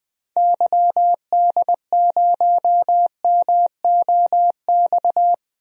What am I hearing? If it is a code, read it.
Morse code, heard as a keyed tone, about 20 wpm, 700 Hz. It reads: YD0MOX
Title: DXCC-100 Call Signs - 6 Characters - 20wpm